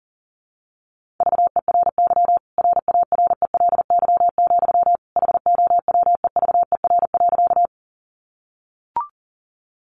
Morse code, heard as a keyed tone, about 37 wpm, 700 Hz. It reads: VERY RARELY, HOWEVER.
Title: Sentences from Top 1400 Words 40wpm